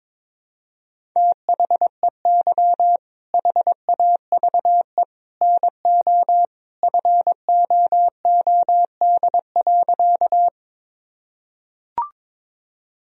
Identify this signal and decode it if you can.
Morse code, heard as a keyed tone, about 22 wpm, 700 Hz. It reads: THEY HAVE NO FOOD.
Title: Sentences from Top 1000 Words - Review Part 1 - 22wpm